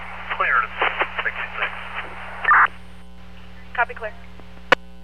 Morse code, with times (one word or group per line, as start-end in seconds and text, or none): none